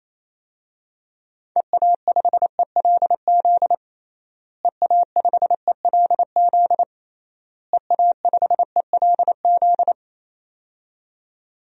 Morse code, heard as a keyed tone, about 28 wpm, 700 Hz. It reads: EA5ELZ EA5ELZ EA5ELZ
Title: DXCC-100 Call Signs - 6 Characters - Repeated 3x - 28wpm